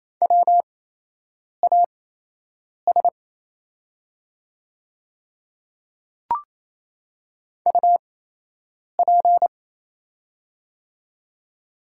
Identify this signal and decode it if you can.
Morse code, heard as a keyed tone, about 28 wpm, spaced as for 8 wpm, 700 Hz. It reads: WAS UP
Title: 8x Character Spacing - Top 200 Words - 28wpm